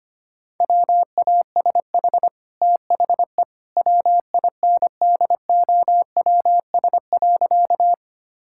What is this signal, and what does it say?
Morse code, heard as a keyed tone, about 25 wpm, 700 Hz. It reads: WASH THE WINDOWS.
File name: Easy Sentences with Hints - 25wpm